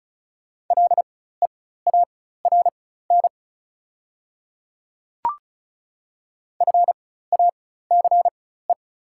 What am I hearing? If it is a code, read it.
Morse code, heard as a keyed tone, about 36 wpm, spaced as for 16 wpm, 690 Hz. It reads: LEARN FACE
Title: 4x Character Spacing - Top 300 Words - 35wpm